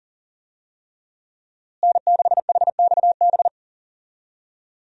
Morse code, heard as a keyed tone, about 37 wpm, 700 Hz. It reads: N6HXB
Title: USA Call Signs - 5 Characters - Repeated 3x - 40wpm